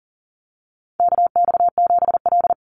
Morse code, heard as a keyed tone, about 37 wpm, 700 Hz. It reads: KX7L
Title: NA SOTA Chaser Callsigns 40wpm